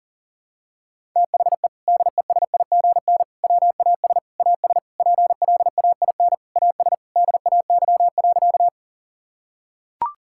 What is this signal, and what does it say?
Morse code, heard as a keyed tone, about 40 wpm, 700 Hz. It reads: THE DESIGN WAS AS PLAIN AS DAY.
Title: Sentences from Top 1000 Words - Review Part 5 - 40wpm